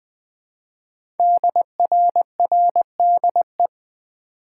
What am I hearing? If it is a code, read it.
Morse code, heard as a keyed tone, about 20 wpm, 700 Hz. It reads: DRRDE
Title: Code Groups - 5 Letters - 20wpm